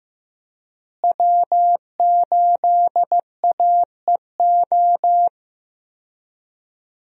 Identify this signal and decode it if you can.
Morse code, heard as a keyed tone, about 15 wpm, 700 Hz. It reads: W8AEO
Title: USA Call Signs - 5 Characters - Repeated 2x - 15wpm